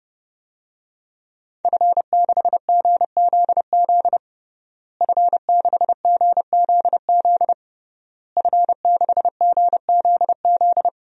Morse code, heard as a keyed tone, about 30 wpm, 700 Hz. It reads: F6GZZ F6GZZ F6GZZ
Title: DXCC-100 Call Signs - 5 Characters - Repeated 3x - 30wpm